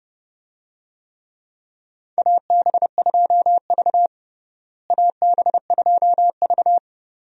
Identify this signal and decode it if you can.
Morse code, heard as a keyed tone, about 30 wpm, 700 Hz. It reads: AB2V AB2V
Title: USA Call Signs - 4 Characters - Repeated 2x - 30wpm